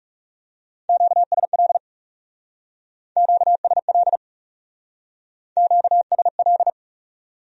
Morse code, heard as a keyed tone, about 39 wpm, 700 Hz. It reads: QSL QSL QSL